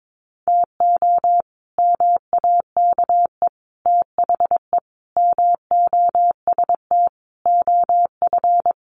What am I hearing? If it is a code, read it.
Morse code, heard as a keyed tone, about 22 wpm, 700 Hz. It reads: TO MAKE THE MOST OF
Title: Sets of 5 Words Encore2 22wpm